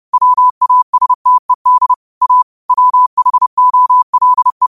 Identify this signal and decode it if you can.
Morse code, heard as a keyed tone, about 30 wpm, 1000 Hz. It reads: WASTED A WHOLE
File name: qotdfiles / 30wpm